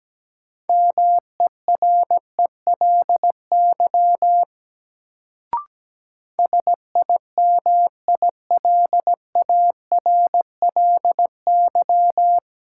Morse code, heard as a keyed tone, about 17 wpm, 700 Hz. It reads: MERELY SIMILARLY
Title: Sentences from Top 1400 Words - New Word Drill - 17wpm